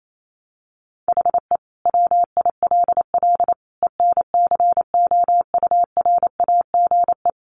Morse code, heard as a keyed tone, about 28 wpm, 700 Hz. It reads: HE WILL ENCOURAGE
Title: Sentences from Top 1400 Words 28wpm